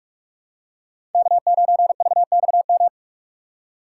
Morse code, heard as a keyed tone, about 44 wpm, 700 Hz. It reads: K9VXM
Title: USA Call Signs - 5 Characters - 3x Character Spacing - 45wpm